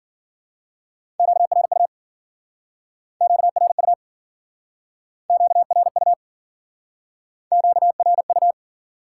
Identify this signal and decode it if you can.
Morse code, heard as a keyed tone, about 52 wpm, 700 Hz. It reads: QRU QRU QRU QRU